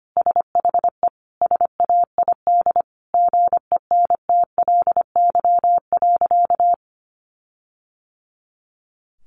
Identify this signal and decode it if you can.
Morse code, heard as a keyed tone, about 25 wpm, 700 Hz. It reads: SHE SAID GENTLY.
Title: Sentences from Top 1500 Words 25wpm